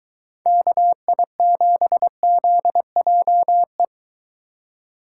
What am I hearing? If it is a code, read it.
Morse code, heard as a keyed tone, about 23 wpm, 700 Hz. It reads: KI7ZJE